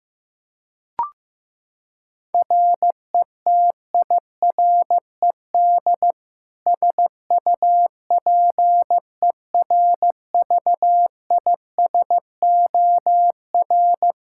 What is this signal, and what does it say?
Morse code, heard as a keyed tone, about 15 wpm, 700 Hz. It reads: RETIRED SUPERVISOR
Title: K1USN SST QSO Practice 15wpm